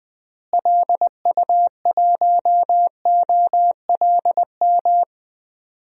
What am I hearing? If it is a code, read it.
Morse code, heard as a keyed tone, about 20 wpm, 700 Hz. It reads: LU1OLM
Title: DXCC-100 Call Signs - 6 Characters - Repeated 3x - 20wpm